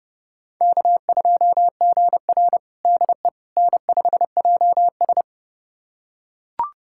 Morse code, heard as a keyed tone, about 30 wpm, 700 Hz. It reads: K2GR DE N5JS
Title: USA Callsign DE Callsign - 4 Characters - 30wpm